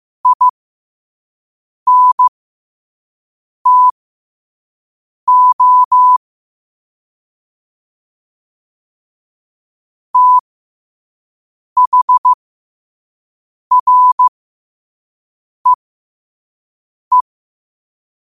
Morse code, heard as a keyed tone, about 15 wpm, spaced as for 5 wpm, 1000 Hz. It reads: INTO THREE